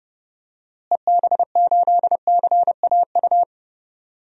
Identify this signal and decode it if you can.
Morse code, heard as a keyed tone, about 30 wpm, 700 Hz. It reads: EB8CAU
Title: DXCC-100 Call Signs - 6 Characters - Repeated 3x - 30wpm